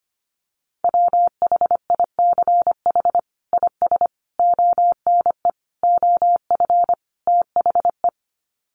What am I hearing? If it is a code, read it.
Morse code, heard as a keyed tone, about 25 wpm, 700 Hz. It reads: WHICH IS ONE OF THE
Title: Sets of 5 Words Encore 25wpm